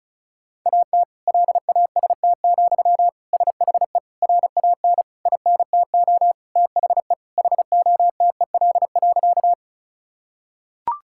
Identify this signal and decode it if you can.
Morse code, heard as a keyed tone, about 36 wpm, 700 Hz. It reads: AT LAST, SHE RAN INTO THE HOTEL.
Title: Sentences from Top 1600 Words 35wpm